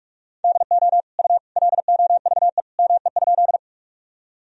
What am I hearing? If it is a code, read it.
Morse code, heard as a keyed tone, about 43 wpm, 700 Hz. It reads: DO U LOVE ME?
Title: Sentences from Top 400 Words and Abbreviations 45wpm